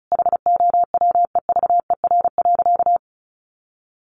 Morse code, heard as a keyed tone, about 34 wpm, 700 Hz. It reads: HOWEVER.
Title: Sentences from Top 1500 Words 35wpm